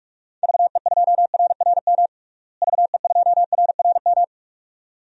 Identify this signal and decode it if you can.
Morse code, heard as a keyed tone, about 41 wpm, 700 Hz. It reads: VE2RRM VE2RRM
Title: DXCC-100 Call Signs - 6 Characters - Repeated 2x - 45wpm